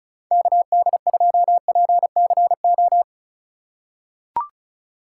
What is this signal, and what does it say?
Morse code, heard as a keyed tone, about 36 wpm, 700 Hz. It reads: KD2PCO
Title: USA Call Signs - 3x Character Spacing - 35wpm